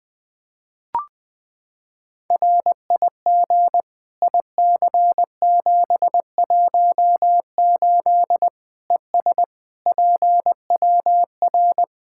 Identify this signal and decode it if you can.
Morse code, heard as a keyed tone, about 20 wpm, 700 Hz. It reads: RIG IC718 ES PWR
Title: Practice QSO - Exchange 2 - Info - 20wpm